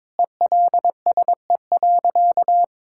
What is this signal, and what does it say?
Morse code, heard as a keyed tone, about 22 wpm, 700 Hz. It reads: ELSE.
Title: Sentences from Top 1600 Words 22wpm